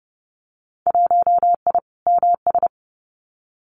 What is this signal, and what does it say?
Morse code, heard as a keyed tone, about 30 wpm, 700 Hz. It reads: 1I MS
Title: Winter Field Day Exchanges Encore 30wpm